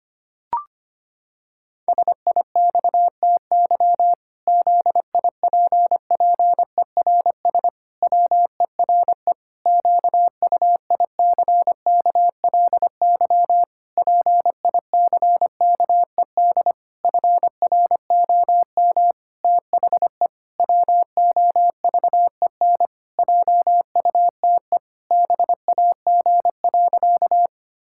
Morse code, heard as a keyed tone, about 25 wpm, 700 Hz. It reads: SIXTY ZIPPERS WERE QUICKLY PICKED FROM THE WOVEN JUTE BAG.
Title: Pangram Challenge v2 - 25wpm